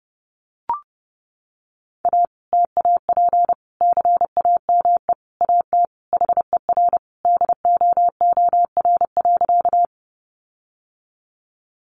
Morse code, heard as a keyed tone, about 30 wpm, 700 Hz. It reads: A TAP CAME AT HER DOOR.
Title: Sentences from Top 1400 Words 30wpm